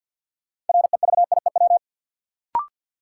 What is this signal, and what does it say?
Morse code, heard as a keyed tone, about 50 wpm, 700 Hz. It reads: REVIEW
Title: Sentences from Top 900 Words - New Word Drill - 50wpm